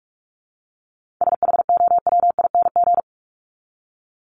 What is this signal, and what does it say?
Morse code, heard as a keyed tone, about 44 wpm, 700 Hz. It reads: SHOWING
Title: ING Suffix - Top 100 Words with Base Words 45wpm